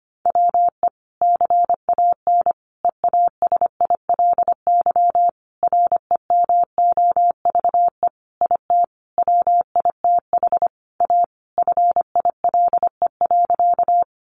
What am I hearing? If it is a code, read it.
Morse code, heard as a keyed tone, about 25 wpm, 700 Hz. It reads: WE CAN EASILY REMOVE IT WITH A FILE.
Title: Sentences from Top 1200 Words 25wpm